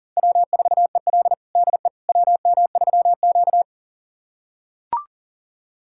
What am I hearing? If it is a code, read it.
Morse code, heard as a keyed tone, about 37 wpm, 700 Hz. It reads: W4EL DE WM3Q
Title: USA Callsign DE Callsign - 4 Characters - 40wpm